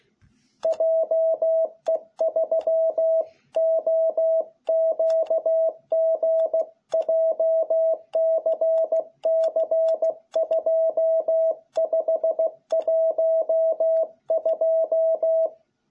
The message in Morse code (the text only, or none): JE3OQG JCC2512